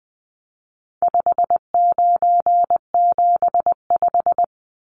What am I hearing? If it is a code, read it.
Morse code, heard as a keyed tone, about 19 wpm, 700 Hz. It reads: 5975
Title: Numbers - 4-Digit 20wpm